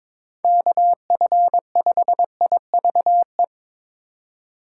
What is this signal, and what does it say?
Morse code, heard as a keyed tone, about 22 wpm, 700 Hz. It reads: KF5IVE
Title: USA Call Signs - 6 Characters - 2x Character Spacing - 22wpm